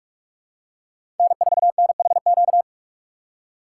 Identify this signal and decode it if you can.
Morse code, heard as a keyed tone, about 44 wpm, 700 Hz. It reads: N 4 N H Q
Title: USA Call Signs - 8x Character Spacing - 45wpm